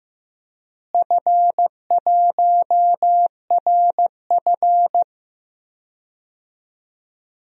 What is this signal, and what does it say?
Morse code, heard as a keyed tone, about 15 wpm, 700 Hz. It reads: F1RF
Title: DXCC-100 Call Signs - 4 Characters - Repeated 2x - 15wpm